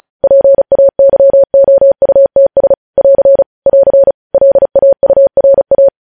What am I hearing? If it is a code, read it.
Morse code, heard as a keyed tone, about 36 wpm, 550 Hz. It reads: PAYOUTS <AR> <AR> LAURA
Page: News headlines in Morse code at 35 wpm